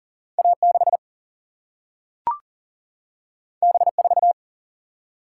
Morse code, heard as a keyed tone, about 40 wpm, 700 Hz. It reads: A 6 B 4
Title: Single-Letter Single-Number 40wpm